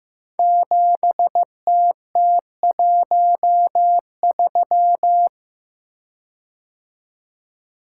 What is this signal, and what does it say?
Morse code, heard as a keyed tone, about 15 wpm, 700 Hz. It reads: 7TT13
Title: Code Groups - 5 Letter:Numbers - 15wpm